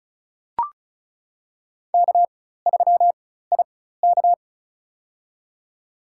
Morse code, heard as a keyed tone, about 34 wpm, spaced as for 16 wpm, 700 Hz. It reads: K3IK
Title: USA Call Signs - 4 Characters - 4x Character Spacing - 35wpm